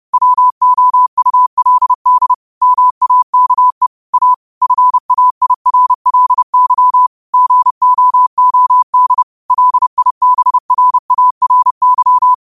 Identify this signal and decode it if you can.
Morse code, heard as a keyed tone, about 30 wpm, 1000 Hz. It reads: WOULD MAKE A FAIRLY GOOD LIBRARY
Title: 30 WPM morse code quotes for Thu, 29 Feb 2024